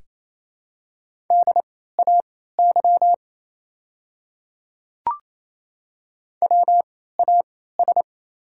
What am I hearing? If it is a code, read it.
Morse code, heard as a keyed tone, about 29 wpm, spaced as for 16 wpm, 690 Hz. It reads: DAY WAS
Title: 3x Character Spacing - Top 100 Words - 28wpm